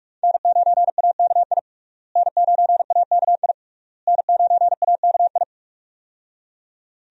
Morse code, heard as a keyed tone, about 42 wpm, 700 Hz. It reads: N9AKI N9AKI N9AKI
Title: USA Call Signs - 5 Characters - Repeated 3x - 45wpm